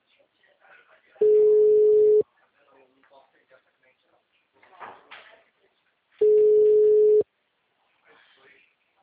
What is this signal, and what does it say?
Ringback tone (Europe, 425 Hz), with speech.